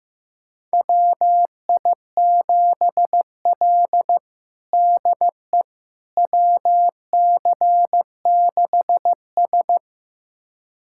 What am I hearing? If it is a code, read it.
Morse code, heard as a keyed tone, about 15 wpm, 700 Hz. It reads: WI7L DE WC6S
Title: USA Callsign DE Callsign - 4 Characters - 15wpm